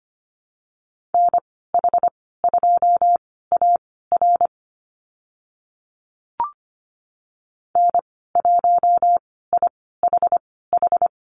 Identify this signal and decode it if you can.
Morse code, heard as a keyed tone, about 25 wpm, spaced as for 16 wpm, 700 Hz. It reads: NH2AR N1IHH
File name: Lesson 15g - Number 5 - Call Signs 25wpm